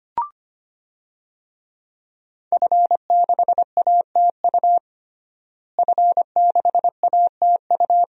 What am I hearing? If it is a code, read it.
Morse code, heard as a keyed tone, about 25 wpm, 700 Hz. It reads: F6ATU F6ATU
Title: DXCC-100 Call Signs - 5 Characters - Repeated 3x - 25wpm